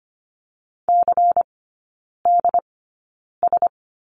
Morse code, heard as a keyed tone, about 25 wpm, spaced as for 9 wpm, 700 Hz. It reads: CDS
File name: course / 25:10wpm / 093 Lesson 16e - Letter c - Three Characters 25wpm